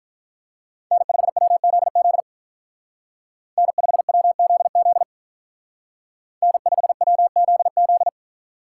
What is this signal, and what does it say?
Morse code, heard as a keyed tone, about 46 wpm, 700 Hz. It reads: N5WZZ N5WZZ N5WZZ